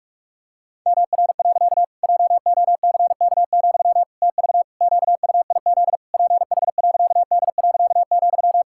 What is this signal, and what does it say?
Morse code, heard as a keyed tone, about 46 wpm, 700 Hz. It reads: MR. JOCK, TV QUIZ PH.D.,
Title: Pangram Challenge v2 - 45wpm